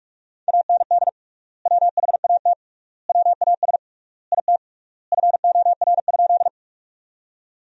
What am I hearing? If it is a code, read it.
Morse code, heard as a keyed tone, about 46 wpm, 700 Hz. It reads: AND WHAT WAS IT FOR?
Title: 3x Word-Spacing - Sentences from Top 100 Words - 45wpm